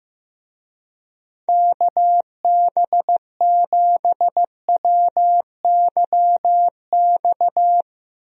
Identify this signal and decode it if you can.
Morse code, heard as a keyed tone, about 15 wpm, 700 Hz. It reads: KB7WYX
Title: USA Call Signs - 6 Characters - Repeated 2x - 15wpm